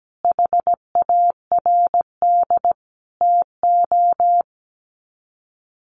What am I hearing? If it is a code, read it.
Morse code, heard as a keyed tone, about 17 wpm, 700 Hz. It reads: HARD TO
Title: Sets of 5 Words Encore2 17wpm